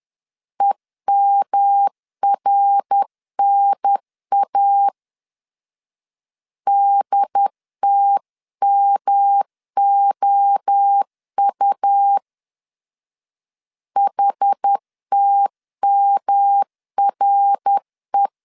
Morse code, heard as a keyed tone, about 10 wpm, 800 Hz. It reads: EMRNA DTMOU HTMRE
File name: 10wpm